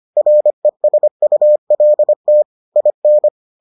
Morse code, heard as a keyed tone, about 25 wpm, 600 Hz. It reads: RESULT IN